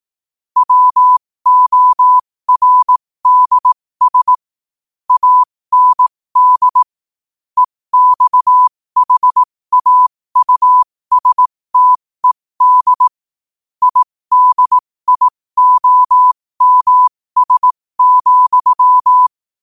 Morse code, heard as a keyed tone, about 18 wpm, spaced as for 15 wpm, 1000 Hz. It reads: WORDS AND EXHAUSTED IDIOMS,
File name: qotdfiles / 15wpm